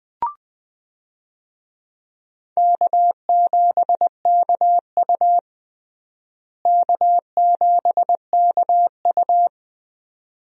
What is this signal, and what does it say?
Morse code, heard as a keyed tone, about 21 wpm, 700 Hz. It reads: K7KU K7KU
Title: USA Call Signs - 4 Characters - Repeated 2x - 20wpm